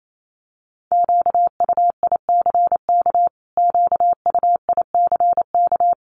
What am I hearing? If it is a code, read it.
Morse code, heard as a keyed tone, about 28 wpm, 700 Hz. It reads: QUICK QUICK
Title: Top 500 Words Repeated 28wpm